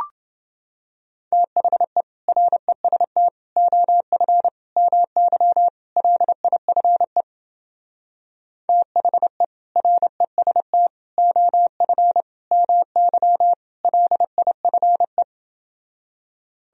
Morse code, heard as a keyed tone, about 29 wpm, 700 Hz. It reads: THE REST OF MY LIFE THE REST OF MY LIFE